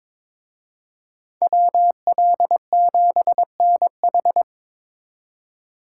Morse code, heard as a keyed tone, about 25 wpm, 700 Hz. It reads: WL7NH
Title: Speed Racing - USA Call Signs 22wpm